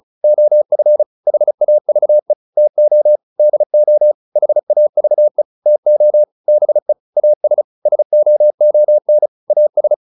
Morse code, heard as a keyed tone, about 34 wpm, 600 Hz. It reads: OF HAVE TO DO HAVE TO BE AS SOON AS